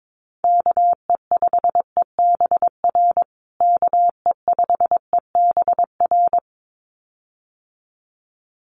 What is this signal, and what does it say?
Morse code, heard as a keyed tone, about 22 wpm, 700 Hz. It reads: KE5EBR KE5EBR
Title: USA Call Signs Repeated 2x 22wpm